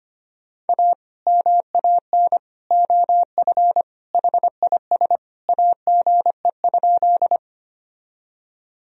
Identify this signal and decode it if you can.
Morse code, heard as a keyed tone, about 25 wpm, 700 Hz. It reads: A MAN OF HIS AGE?
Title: Sentences from Top 1000 Words - Review Part 3 - 25wpm